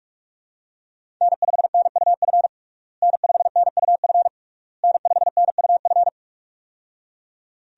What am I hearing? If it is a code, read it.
Morse code, heard as a keyed tone, about 46 wpm, 700 Hz. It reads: N5NUF N5NUF N5NUF
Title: USA Call Signs - 5 Characters - Repeated 3x - 45wpm